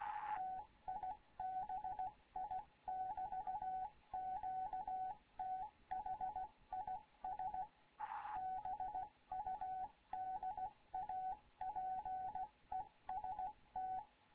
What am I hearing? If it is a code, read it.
Morse code, heard as a keyed tone, about 16 wpm, 750 Hz. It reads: TIBI <BT> QTH IS BUDAPEST
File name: Morze-HA2MN / hangp / 2 11 GYAKORLAT